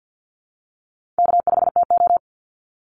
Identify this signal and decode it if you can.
Morse code, heard as a keyed tone, about 50 wpm, 700 Hz. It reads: K5TO
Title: Call Signs - 50wpm